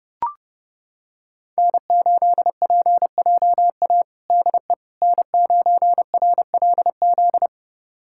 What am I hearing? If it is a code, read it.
Morse code, heard as a keyed tone, about 30 wpm, 700 Hz. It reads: N8PJA DE N9RLZ
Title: USA Callsign DE Callsign - 5 Characters - 30wpm